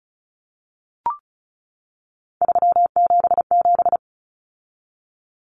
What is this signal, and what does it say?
Morse code, heard as a keyed tone, about 34 wpm, 700 Hz. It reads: 377
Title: Numbers - 3-Digit 35wpm